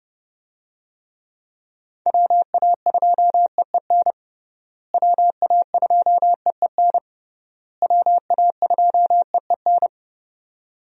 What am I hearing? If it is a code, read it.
Morse code, heard as a keyed tone, about 30 wpm, 700 Hz. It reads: WA2EEN WA2EEN WA2EEN
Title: USA Call Signs - 6 Characters - Repeated 3x - 30wpm